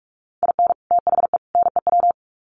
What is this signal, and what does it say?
Morse code, heard as a keyed tone, about 44 wpm, 700 Hz. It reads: IN THE NEW
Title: Sets of 3 Words 45wpm